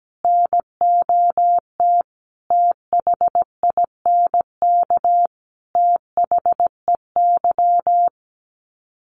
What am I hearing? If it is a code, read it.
Morse code, heard as a keyed tone, about 17 wpm, 700 Hz. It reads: NOT THINK THEY
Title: Sets of 5 Words 17wpm